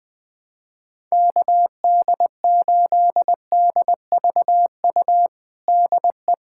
Morse code, heard as a keyed tone, about 20 wpm, 700 Hz. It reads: KD8DVU DE
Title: USA Callsign DE Callsign 20wpm